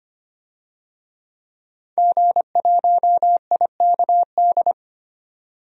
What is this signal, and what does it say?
Morse code, heard as a keyed tone, about 25 wpm, 700 Hz. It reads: G1IKD
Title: DXCC-100 Call Signs - 5 Characters - 25wpm